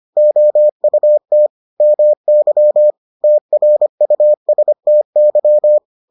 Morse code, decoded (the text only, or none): OUT MY TRUSTY